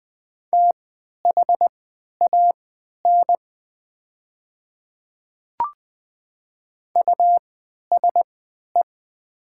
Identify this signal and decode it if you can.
Morse code, heard as a keyed tone, about 20 wpm, spaced as for 11 wpm, 700 Hz. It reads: THAN USE
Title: 3x Character Spacing - Top 300 Words - 20wpm